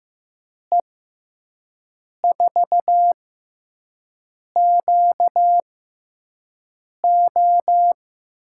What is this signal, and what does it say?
Morse code, heard as a keyed tone, about 15 wpm, spaced as for 5 wpm, 700 Hz. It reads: E4QO
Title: USA Call Signs - 6 Characters - 6x Character Spacing - 15wpm